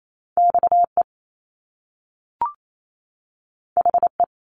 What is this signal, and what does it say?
Morse code, heard as a keyed tone, about 28 wpm, 700 Hz. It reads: XE HE
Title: Top Two Letter Combinations in Words 28wpm